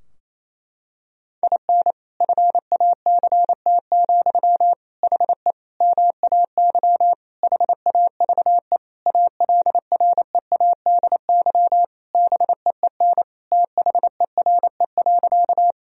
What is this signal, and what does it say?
Morse code, heard as a keyed tone, about 28 wpm, 700 Hz. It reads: IN FACT, HE MAY HAVE ALREADY BEEN THERE.
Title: Sentences from Top 1000 Words - Review Part 3 - 28wpm